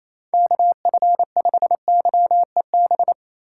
Morse code, heard as a keyed tone, about 28 wpm, 700 Hz. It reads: KF5YEB
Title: USA Call Signs - 6x Character Spacing - 28wpm